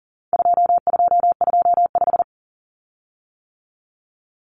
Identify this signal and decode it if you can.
Morse code, heard as a keyed tone, about 40 wpm, 700 Hz. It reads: 2225
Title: Common Cut Numbers - 4-Digit 40wpm